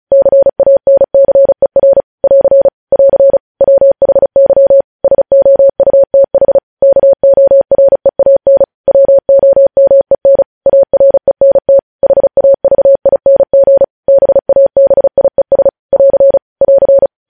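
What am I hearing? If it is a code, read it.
Morse code, heard as a keyed tone, about 36 wpm, 550 Hz. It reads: CANCER <AR> <AR> WHY SOUTH KOREAN WOMEN ARENT HAVING BABIES <AR> <AR>
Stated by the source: Morse news headlines practice at 35 words per minute.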